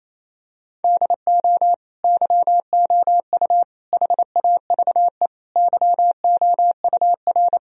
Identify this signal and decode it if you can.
Morse code, heard as a keyed tone, about 27 wpm, 700 Hz. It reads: DO YOU HAVE YOUR
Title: Sentences from Top 1000 Words - Review Part 4 - 28wpm